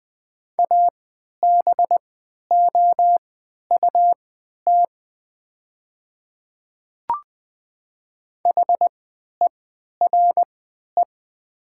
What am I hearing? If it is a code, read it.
Morse code, heard as a keyed tone, about 20 wpm, spaced as for 11 wpm, 700 Hz. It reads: ABOUT HERE
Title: 3x Character Spacing - Top 300 Words - 20wpm